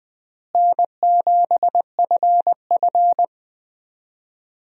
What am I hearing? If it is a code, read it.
Morse code, heard as a keyed tone, about 20 wpm, 700 Hz. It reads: N7FF
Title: USA Call Signs - 4 Characters - Repeated 2x - 20wpm